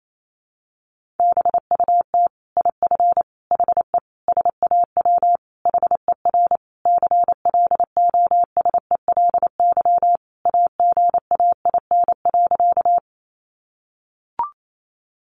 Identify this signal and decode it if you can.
Morse code, heard as a keyed tone, about 28 wpm, 700 Hz. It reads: BUT IF HE SAW HER CLOSELY AGAIN.
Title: Sentences from Top 1400 Words 28wpm